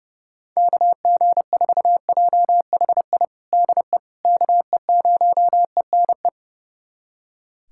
Morse code, heard as a keyed tone, about 32 wpm, 700 Hz. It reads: KG4JHI DE KE0ENE
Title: USA Callsign DE Callsign 30wpm